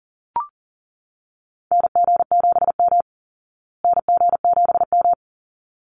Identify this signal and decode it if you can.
Morse code, heard as a keyed tone, about 40 wpm, 700 Hz. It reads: NG7M NG7M
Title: Call Signs Repeated 3x - 40wpm